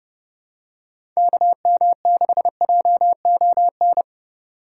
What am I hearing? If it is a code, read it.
Morse code, heard as a keyed tone, about 30 wpm, 700 Hz. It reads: KM6JON
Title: USA Call Signs - 6 Characters - 4x Character Spacing - 30wpm